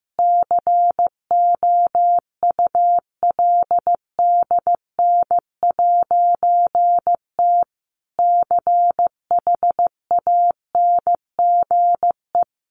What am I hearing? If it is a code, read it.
Morse code, heard as a keyed tone, about 15 wpm, 700 Hz. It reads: COULDN'T CHANGE
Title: A Sound Of Thunder 15wpm